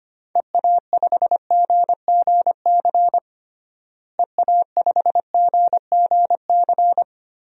Morse code, heard as a keyed tone, about 25 wpm, 700 Hz. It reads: EA5GGC EA5GGC
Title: DXCC-100 Call Signs - 6 Characters - Repeated 2x - 25wpm